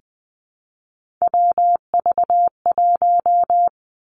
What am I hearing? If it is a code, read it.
Morse code, heard as a keyed tone, about 19 wpm, 700 Hz. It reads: WV1
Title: Three Challenging Letter-Number Combinations 20wpm